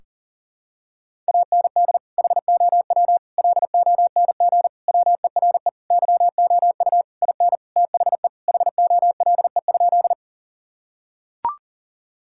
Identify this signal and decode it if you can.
Morse code, heard as a keyed tone, about 40 wpm, 700 Hz. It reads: AND HOW LONG WERE YOU IN THE HOLE?
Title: Sentences from Top 1600 Words 40wpm